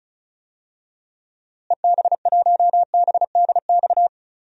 Morse code, heard as a keyed tone, about 36 wpm, 700 Hz. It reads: EB1BDX
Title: DXCC-100 Call Signs - 6 Characters - Repeated 2x - 35wpm